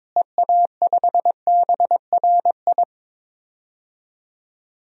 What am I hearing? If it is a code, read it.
Morse code, heard as a keyed tone, about 21 wpm, 700 Hz. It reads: EA5BRI
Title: DXCC-100 Call Signs - 6 Characters - Repeated 3x - 22wpm